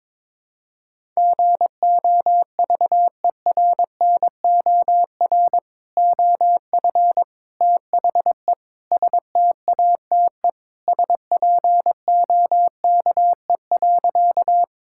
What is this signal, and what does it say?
Morse code, heard as a keyed tone, about 21 wpm, 700 Hz. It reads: GOVERNOR OF THE STATE SPOKE.
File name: practice / 22wpm / Sentences with 3-Word Phrases - 22wpm